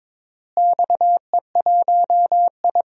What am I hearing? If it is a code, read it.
Morse code, heard as a keyed tone, about 22 wpm, 700 Hz. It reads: XE1I
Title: DXCC-100 Call Signs - 4 Characters - Repeated 3x - 22wpm